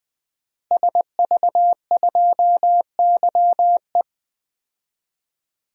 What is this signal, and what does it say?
Morse code, heard as a keyed tone, about 20 wpm, 700 Hz. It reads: SV2YE
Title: DXCC-100 Call Signs - 5 Characters - Repeated 3x - 20wpm